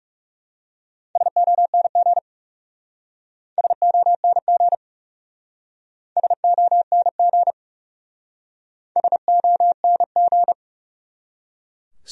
Morse code, heard as a keyed tone, about 37 wpm, 700 Hz. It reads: SONG SONG SONG SONG
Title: Speed-Racing Top 500 Words 30wpm